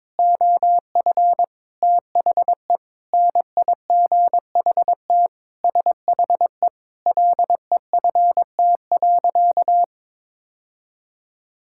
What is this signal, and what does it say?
Morse code, heard as a keyed tone, about 21 wpm, 700 Hz. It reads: OF THE NIGHT SHE LEFT.
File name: practice / 22wpm / Sentences with 3-Word Phrases - 22wpm